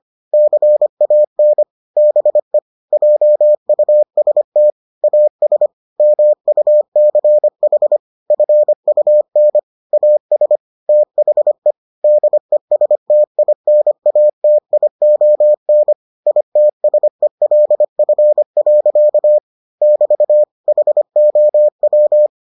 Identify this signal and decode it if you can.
Morse code, heard as a keyed tone, about 25 wpm, 600 Hz. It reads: CAN BE JUST AS MUCH FUN AS THE DESTINATION ITSELF. <BT> HOW